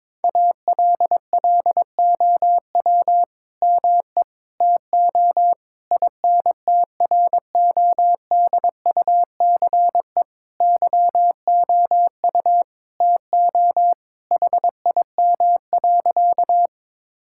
Morse code, heard as a keyed tone, about 22 wpm, 700 Hz. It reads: ALLOW ME TO INTRODUCE YOU TO HIM.
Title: Sentences from Top 1600 Words 22wpm